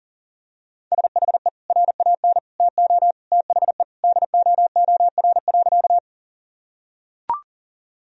Morse code, heard as a keyed tone, about 40 wpm, 700 Hz. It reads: SHE RAN TO THE DOOR.
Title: Sentences from Top 1000 Words - Review Part 3 - 40wpm